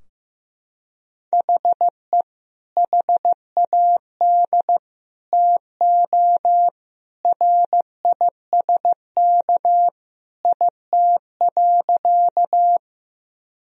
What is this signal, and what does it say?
Morse code, heard as a keyed tone, about 15 wpm, 700 Hz. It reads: HE HAD TO RISK IT.
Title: Sentences from Top 1000 Words - Review Part 4 - 15wpm